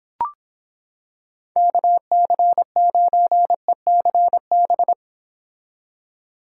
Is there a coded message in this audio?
Yes: Morse code from 1.6 to 4.9 s, heard as a keyed tone, about 26 wpm, 700 Hz. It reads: KC9ECB